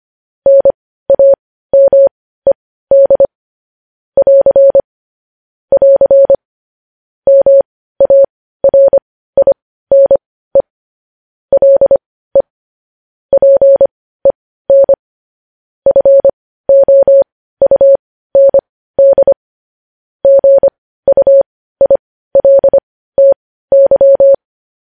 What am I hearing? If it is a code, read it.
Morse code, heard as a keyed tone, about 25 wpm, spaced as for 15 wpm, 550 Hz. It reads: NAMED <AR> <AR> MARINE LE PEN FOUND GUILTY
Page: news-headlines-in-morse-code-at-15-wpm